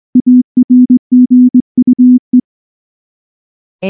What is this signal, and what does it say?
Morse code, heard as a keyed tone, about 18 wpm, 250 Hz. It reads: ARGUE